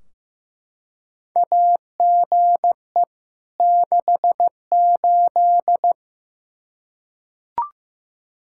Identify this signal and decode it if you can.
Morse code, heard as a keyed tone, about 15 wpm, 700 Hz. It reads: AGE 68
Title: Practice QSO - Exchange 3 - More Info - 15wpm